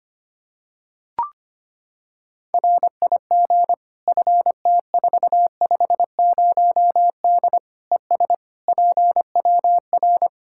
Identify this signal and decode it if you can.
Morse code, heard as a keyed tone, about 25 wpm, 700 Hz. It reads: RIG FT450D ES PWR
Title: Practice QSO - Exchange 2 - Info - 25wpm